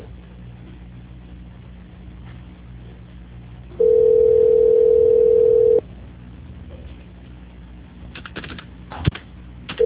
Ringback tone (North America), 440 + 480 Hz.